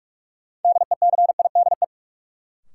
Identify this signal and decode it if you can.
Morse code, heard as a keyed tone, about 46 wpm, 700 Hz. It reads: DECIDE
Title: Sentences from Top 500 Words - New Word Drill - 45wpm